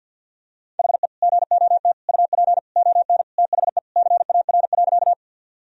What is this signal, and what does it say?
Morse code, heard as a keyed tone, about 51 wpm, 700 Hz. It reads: HE GOT UP ON THE CAR.
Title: Sentences from Top 1000 Words - Review Part 1 - 50wpm